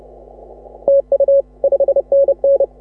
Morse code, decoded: TU 5NN

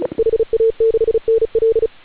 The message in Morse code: HA6NL

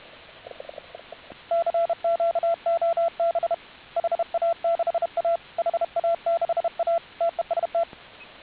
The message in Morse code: CQ OB HA6A HA6A TEST

